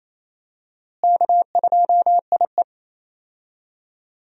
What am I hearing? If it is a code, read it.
Morse code, heard as a keyed tone, about 28 wpm, 700 Hz. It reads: K2IE
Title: USA Call Signs - 4 Characters - Repeated 3x - 28wpm